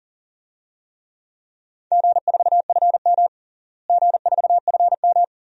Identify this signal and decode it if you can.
Morse code, heard as a keyed tone, about 40 wpm, 700 Hz. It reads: G4FM G4FM
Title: DXCC-100 Call Signs - 4 Characters - Repeated 2x - 40wpm